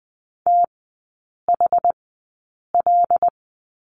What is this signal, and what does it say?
Morse code, heard as a keyed tone, about 20 wpm, spaced as for 8 wpm, 700 Hz. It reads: THL
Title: 100 Lesson 17e - Letter u - Three Characters 20wpm